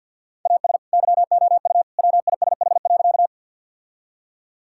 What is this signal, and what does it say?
Morse code, heard as a keyed tone, about 50 wpm, 700 Hz. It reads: AS YOU WISH.
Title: Sentences from Top 1600 Words 50wpm